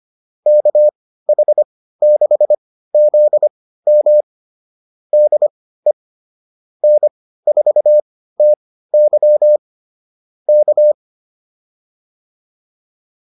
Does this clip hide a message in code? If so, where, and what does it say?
Yes: Morse code from 0.5 to 10.9 s, heard as a keyed tone, about 25 wpm, spaced as for 15 wpm, 600 Hz. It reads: KH6ZM DE N4TY K